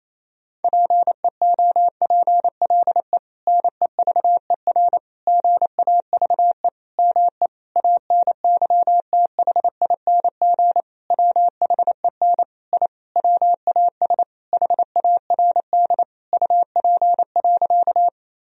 Morse code, heard as a keyed tone, about 28 wpm, 700 Hz. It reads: PEOPLE NEVER GAVE ME ANYTHING WHEN I WAS HARD UP.
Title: Sentences from Top 1000 Words - Review Part 1 - 28wpm